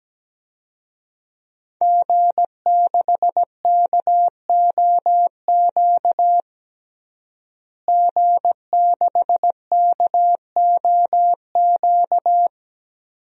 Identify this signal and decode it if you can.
Morse code, heard as a keyed tone, about 17 wpm, 700 Hz. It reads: G6KOQ G6KOQ
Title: DXCC-100 Call Signs - 5 Characters - Repeated 2x - 17wpm